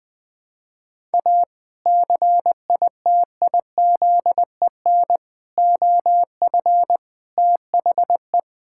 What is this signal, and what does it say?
Morse code, heard as a keyed tone, about 21 wpm, 700 Hz. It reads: A CITIZEN OF THE
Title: Sentences with 3-Word Phrases - 20wpm